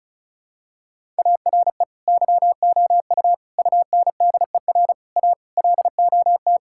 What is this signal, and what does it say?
Morse code, heard as a keyed tone, about 36 wpm, 700 Hz. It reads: ARE YOU UNDER A LOT
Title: Sentences from Top 1000 Words - Review Part 6 - 35wpm